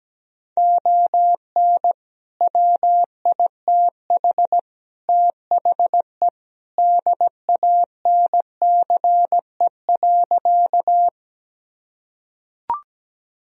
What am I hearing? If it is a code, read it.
Morse code, heard as a keyed tone, about 17 wpm, 700 Hz. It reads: ON WITH THE DANCE.
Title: Sentences from Top 1000 Words - Review Part 4 - 17wpm